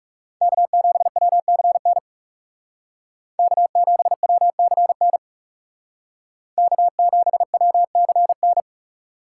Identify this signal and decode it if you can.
Morse code, heard as a keyed tone, about 36 wpm, 700 Hz. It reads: K7WCN K7WCN K7WCN
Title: Speed Racing - USA 1x3 Call Signs 30wpm